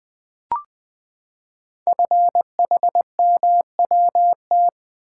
Morse code, heard as a keyed tone, about 20 wpm, 700 Hz. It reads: FHMWT
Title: Code Groups - 5 Letters - 20wpm